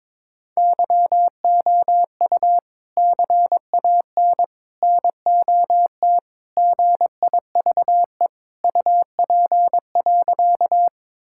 Morse code, heard as a keyed tone, about 22 wpm, 700 Hz. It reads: YOU CAN NOT GIVE UP.
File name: practice / 22wpm / Sentences with 3-Word Phrases - 22wpm